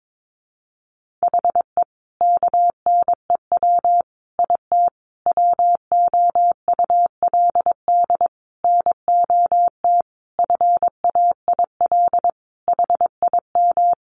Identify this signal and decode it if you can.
Morse code, heard as a keyed tone, about 22 wpm, 700 Hz. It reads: HE KNEW IT WOULD NOT FAIL HIM
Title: Sentences from Top 1100 Words 22wpm